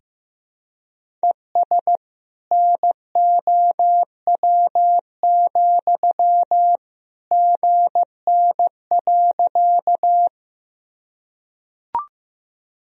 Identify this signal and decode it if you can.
Morse code, heard as a keyed tone, about 15 wpm, 700 Hz. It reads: ES NOW, GN.
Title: Sentences from Top 300 Words and Abbreviations 15wpm